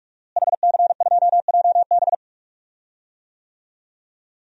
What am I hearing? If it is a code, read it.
Morse code, heard as a keyed tone, about 46 wpm, 700 Hz. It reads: HC2JB
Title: DXCC-100 Call Signs - 5 Characters - Repeated 2x - 45wpm